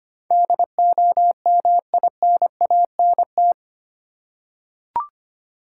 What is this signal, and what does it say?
Morse code, heard as a keyed tone, about 25 wpm, 700 Hz. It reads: DOMINANT
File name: Eight Letter Words - 25wpm